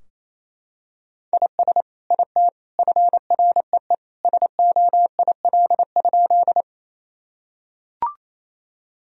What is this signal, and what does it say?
Morse code, heard as a keyed tone, about 28 wpm, 700 Hz. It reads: IS IT FREE SOIL?
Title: Sentences from Top 1000 Words - Review Part 4 - 28wpm